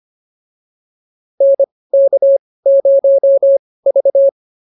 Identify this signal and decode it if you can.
Morse code, heard as a keyed tone, about 25 wpm, spaced as for 18 wpm, 550 Hz. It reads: NK0V